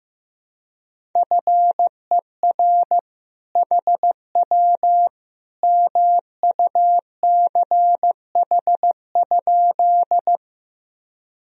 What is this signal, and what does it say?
Morse code, heard as a keyed tone, about 16 wpm, 700 Hz. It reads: FER HW MUCH?
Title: Sentences from Top 200 Words and Abbreviations 15wpm